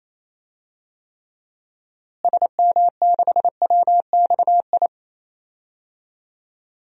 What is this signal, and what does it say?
Morse code, heard as a keyed tone, about 28 wpm, 700 Hz. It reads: SM6WXI
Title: DXCC-100 Call Signs - 6 Characters - Repeated 2x - 28wpm